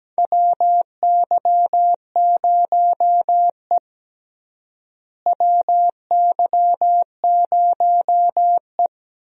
Morse code, heard as a keyed tone, about 17 wpm, 700 Hz. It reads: WY0E WY0E
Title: USA Call Signs - 4 Characters - Repeated 3x - 17wpm